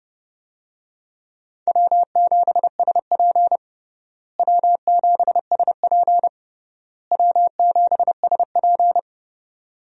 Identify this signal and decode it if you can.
Morse code, heard as a keyed tone, about 31 wpm, 700 Hz. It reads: W7SP W7SP W7SP
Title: USA Call Signs - 4 Characters - Repeated 3x - 30wpm